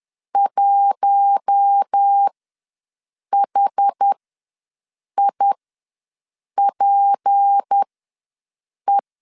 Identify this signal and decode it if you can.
Morse code, heard as a keyed tone, about 10 wpm, spaced as for 6 wpm, 800 Hz. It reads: 1HIPE
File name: LEZIONI / 6 PKX0 / 6 PK ws 10wpm 3745